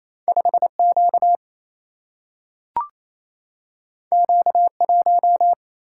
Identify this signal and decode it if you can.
Morse code, heard as a keyed tone, about 28 wpm, 700 Hz. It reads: 5 Q Q 1
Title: Single-Letter Single-Number 28wpm